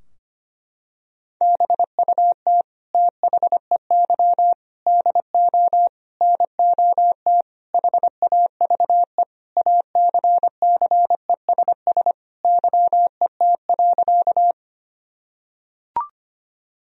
Morse code, heard as a keyed tone, about 25 wpm, 700 Hz. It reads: BUT THEY DO NOT HAVE ACCESS YET.
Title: Sentences from Top 1000 Words - Review Part 4 - 25wpm